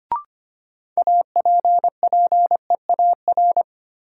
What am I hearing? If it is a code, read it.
Morse code, heard as a keyed tone, about 25 wpm, 700 Hz. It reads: APPEAR
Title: Sentences from Top 1000 Words - Review Part 2 - New Word Drill - 25wpm